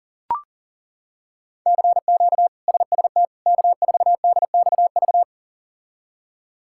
Morse code, heard as a keyed tone, about 40 wpm, 700 Hz. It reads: CQ SST K4DXV
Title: K1USN SST QSO Practice - 40wpm